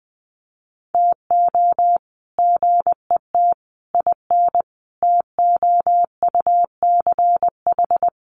Morse code, heard as a keyed tone, about 19 wpm, 700 Hz. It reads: TO GET IN TOUCH
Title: Sets of 5 Words Encore2 20wpm